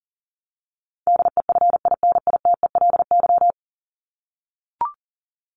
Morse code, heard as a keyed tone, about 40 wpm, 700 Hz. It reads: DEFINITELY
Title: Top 1751-2000 Words 40wpm